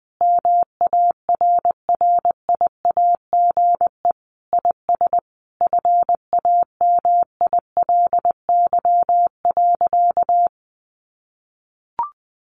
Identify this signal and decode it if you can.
Morse code, heard as a keyed tone, about 20 wpm, 700 Hz. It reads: MARRIAGE IS FAMILY.
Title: Sentences from Top 800 Words 20wpm